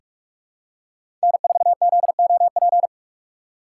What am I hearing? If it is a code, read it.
Morse code, heard as a keyed tone, about 44 wpm, 700 Hz. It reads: N4ZOP
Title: USA Call Signs - 5 Characters - 6x Character Spacing - 45wpm